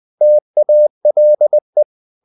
Morse code, heard as a keyed tone, about 20 wpm, 600 Hz. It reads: TALE
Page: Ferie 20 wpm